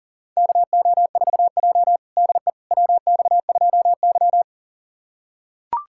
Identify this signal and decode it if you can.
Morse code, heard as a keyed tone, about 40 wpm, 700 Hz. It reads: KO4J DE WX2Y
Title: USA Callsign DE Callsign - 4 Characters - 40wpm